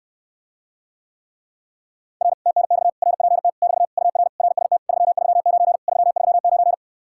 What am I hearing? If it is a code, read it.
Morse code, heard as a keyed tone, about 66 wpm, 700 Hz. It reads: R TNX RPT <BT> UR RST 339 339